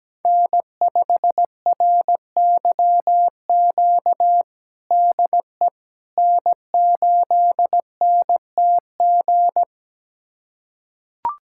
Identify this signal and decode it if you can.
Morse code, heard as a keyed tone, about 17 wpm, 700 Hz. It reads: N5RYQ DE N8NTG
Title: USA Callsign DE Callsign - 5 Characters - 17wpm